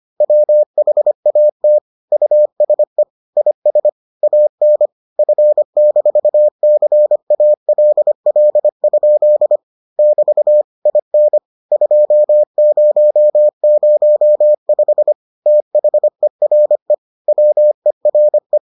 Morse code, heard as a keyed tone, about 25 wpm, 600 Hz. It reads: WHAT USE IS AN F-CALL? <BT> IN 2005 THERE WERE